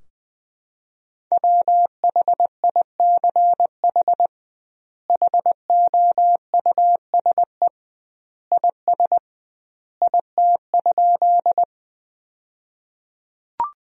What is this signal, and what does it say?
Morse code, heard as a keyed tone, about 21 wpm, 700 Hz. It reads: WHICH HOUSE IS IT?
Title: 2x Word-Spacing - Sentences from Top 200 Words - 20wpm